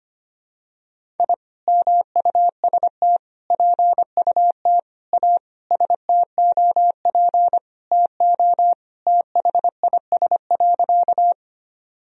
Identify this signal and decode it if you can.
Morse code, heard as a keyed tone, about 25 wpm, 700 Hz. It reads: I MUST PUT A STOP TO THIS.
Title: Sentences from Top 1000 Words - Review Part 1 - 25wpm